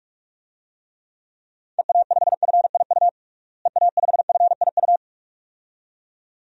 Morse code, heard as a keyed tone, about 46 wpm, 700 Hz. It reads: EA5FIU EA5FIU
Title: DXCC-100 Call Signs - 6 Characters - Repeated 2x - 45wpm